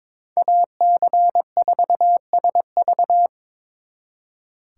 Morse code, heard as a keyed tone, about 26 wpm, 700 Hz. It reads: AC4SV
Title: Speed Racing - USA 2x2 Call Signs 22wpm